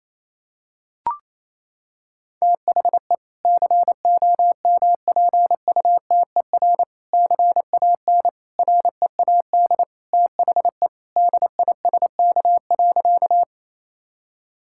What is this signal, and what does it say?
Morse code, heard as a keyed tone, about 27 wpm, 700 Hz. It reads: THE COMPUTER CAN READ THE DISK.
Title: Sentences from Top 1000 Words - Review Part 6 - 28wpm